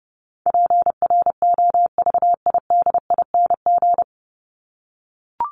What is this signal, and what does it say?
Morse code, heard as a keyed tone, about 30 wpm, 700 Hz. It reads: PROVIDING
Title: ING Suffix - Top 100 Words with Base Words 30wpm